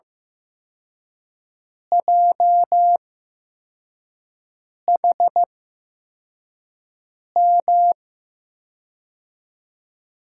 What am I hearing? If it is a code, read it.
Morse code, heard as a keyed tone, about 15 wpm, spaced as for 4 wpm, 700 Hz. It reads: JHM
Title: USA Call Signs - 6 Characters - 8x Character Spacing - 15wpm